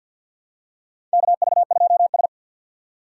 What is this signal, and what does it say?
Morse code, heard as a keyed tone, about 51 wpm, 700 Hz. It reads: KV2S
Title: USA Call Signs - 4 Characters - 2x Character Spacing - 50wpm